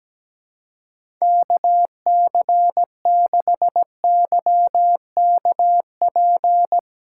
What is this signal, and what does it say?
Morse code, heard as a keyed tone, about 17 wpm, 700 Hz. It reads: KC6YKP
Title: USA Call Signs - 6 Characters - 2x Character Spacing - 17wpm